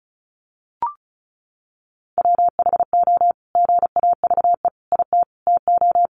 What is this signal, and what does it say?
Morse code, heard as a keyed tone, about 34 wpm, 700 Hz. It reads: WHO GAVE IT TO
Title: Sentences from Top 500 Words 35wpm